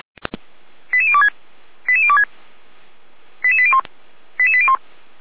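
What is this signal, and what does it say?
Five-tone selective call (ZVEI), heard as quick runs of five tones, tones of about 80 ms. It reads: 88026 88026 88801 88801
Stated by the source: ZVEI five-tone call 89286 received by SDR.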